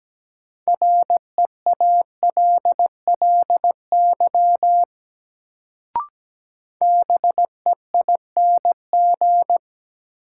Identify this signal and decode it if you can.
Morse code, heard as a keyed tone, about 17 wpm, 700 Hz. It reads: REALLY BEING
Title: Sentences from Top 600 Words - New Word Drill - 17wpm